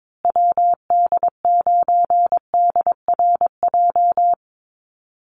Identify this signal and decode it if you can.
Morse code, heard as a keyed tone, about 22 wpm, 690 Hz. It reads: WD9DRJ
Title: USA Call Signs Repeated 3x 22wpm